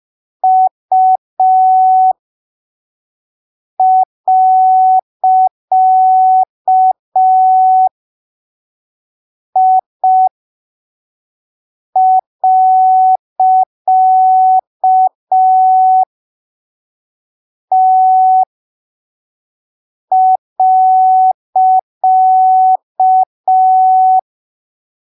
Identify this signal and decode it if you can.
Morse code, heard as a keyed tone, about 5 wpm, spaced as for 3 wpm, 750 Hz. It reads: U.I.T.
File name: arquivos / texto / texto 07 05ppm